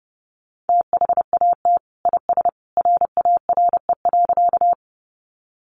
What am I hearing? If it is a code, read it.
Morse code, heard as a keyed tone, about 30 wpm, 700 Hz. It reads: THAT IS RARE.
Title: Sentences from Top 1100 Words 30wpm